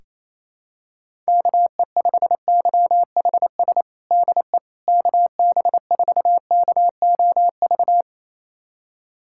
Morse code, heard as a keyed tone, about 28 wpm, 700 Hz. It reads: KE5YHS DE KB4KOV
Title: USA Callsign DE Callsign - 6 Characters - 28wpm